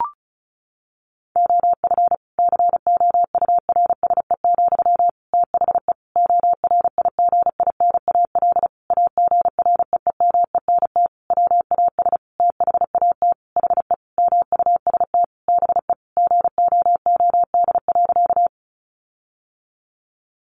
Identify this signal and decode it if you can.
Morse code, heard as a keyed tone, about 34 wpm, 700 Hz. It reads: OF COURSE, THE ORIGINAL AGREEMENT WAS THAT HE MUST BE GOOD.
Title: Sentences from Top 1500 Words 35wpm